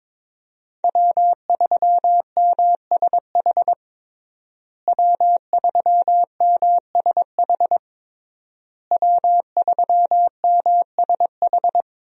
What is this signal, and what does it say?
Morse code, heard as a keyed tone, about 22 wpm, 700 Hz. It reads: W3MSH W3MSH W3MSH
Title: USA Call Signs - 5 Characters - Repeated 3x - 22wpm